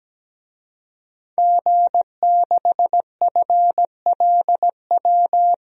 Morse code, heard as a keyed tone, about 17 wpm, 700 Hz. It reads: G6FLW
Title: DXCC-100 Call Signs - 5 Characters - 17wpm